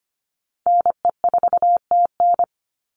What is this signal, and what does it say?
Morse code, heard as a keyed tone, about 25 wpm, 700 Hz. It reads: NE4TN
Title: NA SOTA Chaser Callsigns 25wpm